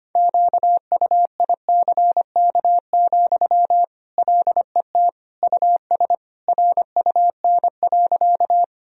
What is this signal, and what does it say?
Morse code, heard as a keyed tone, about 25 wpm, 700 Hz. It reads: QUICK, LET US RUN.
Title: Sentences from Top 1000 Words - Review Part 1 - 25wpm